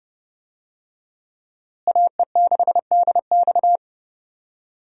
Morse code, heard as a keyed tone, about 30 wpm, 700 Hz. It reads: AE6DX
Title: USA Call Signs - 5 Characters - Repeated 3x - 30wpm